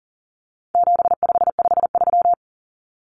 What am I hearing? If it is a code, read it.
Morse code, heard as a keyed tone, about 40 wpm, 700 Hz. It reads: 7553
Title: Numbers - 4-Digit 40wpm